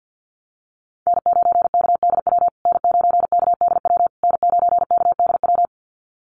Morse code, heard as a keyed tone, about 50 wpm, 700 Hz. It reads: N9KDW N9KDW N9KDW
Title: USA Call Signs Repeated 3x 50wpm